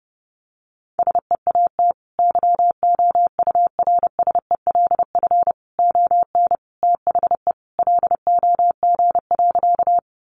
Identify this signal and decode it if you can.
Morse code, heard as a keyed tone, about 30 wpm, 700 Hz. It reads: SEAT YOURSELF ON THE LOG.
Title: Sentences from Top 1500 Words 30wpm